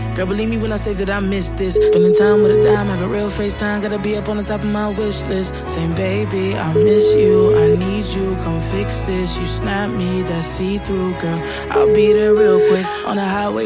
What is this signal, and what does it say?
Ringback tone (Europe, 425 Hz), with speech.